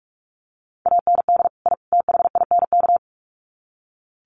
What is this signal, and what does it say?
Morse code, heard as a keyed tone, about 44 wpm, 700 Hz. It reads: AND I THINK
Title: Sets of 3 Words 45wpm